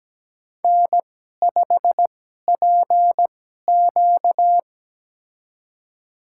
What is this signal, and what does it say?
Morse code, heard as a keyed tone, about 17 wpm, spaced as for 12 wpm, 700 Hz. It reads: N5PQ
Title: USA Call Signs - 2x Character Spacing - 17wpm